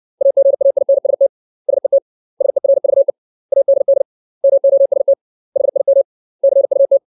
Morse code, heard as a keyed tone, about 60 wpm, 550 Hz. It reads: AGAINST SET SERVE ADD MOST HIM CUT
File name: 632 words / 632 Words 60WPM